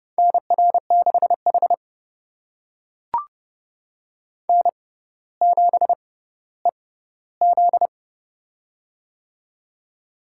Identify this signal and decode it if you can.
Morse code, heard as a keyed tone, about 32 wpm, spaced as for 16 wpm, 700 Hz. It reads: NR6H N7EZ
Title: USA Call Signs - 4 Characters - 6x Character Spacing - 30wpm